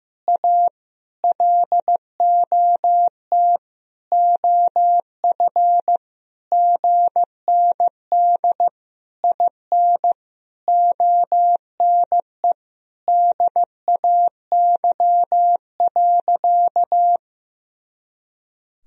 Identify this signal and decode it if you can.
Morse code, heard as a keyed tone, about 16 wpm, 700 Hz. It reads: A LOT OF GND IN ONE DAY.
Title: Sentences from Top 500 Words and Abbreviations 15wpm